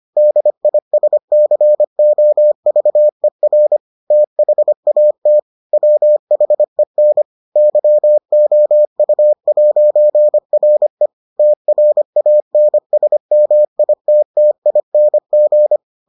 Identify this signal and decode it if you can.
Morse code, heard as a keyed tone, about 25 wpm, 600 Hz. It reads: DISCOVER THAT WHEN YOU'RE TRANSMITTING